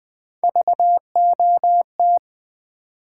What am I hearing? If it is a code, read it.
Morse code, heard as a keyed tone, about 20 wpm, 700 Hz. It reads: VOT
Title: Build A Word - 1 Letter At A Time - 4-Letter Words - 20wpm